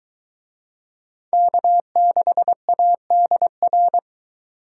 Morse code, heard as a keyed tone, about 23 wpm, 700 Hz. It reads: K6ADR